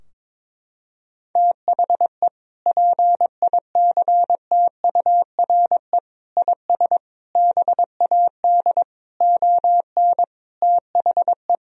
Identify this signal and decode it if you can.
Morse code, heard as a keyed tone, about 21 wpm, 700 Hz. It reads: THE PICTURE IS BAD ON THE
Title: Sentences from Top 1000 Words - Review Part 4 - 22wpm